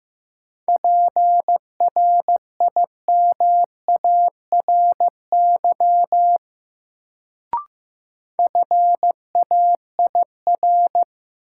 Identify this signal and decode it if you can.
Morse code, heard as a keyed tone, about 15 wpm, 700 Hz. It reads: PRIMARY FAIR
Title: Sentences from Top 1000 Words - Review Part 5 - New Word Drill - 15wpm